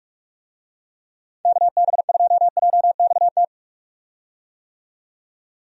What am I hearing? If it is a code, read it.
Morse code, heard as a keyed tone, about 46 wpm, 700 Hz. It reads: KB2JXT
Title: USA Call Signs - 6 Characters - 45wpm